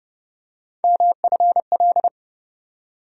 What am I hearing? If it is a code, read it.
Morse code, heard as a keyed tone, about 30 wpm, 700 Hz. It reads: MFL
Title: USA Call Sign Postfixes - 3 Letters 30wpm